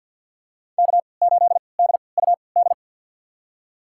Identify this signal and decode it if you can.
Morse code, heard as a keyed tone, about 47 wpm, spaced as for 28 wpm, 700 Hz. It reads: K8DUD
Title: USA Call Signs - 3x Character Spacing - 50wpm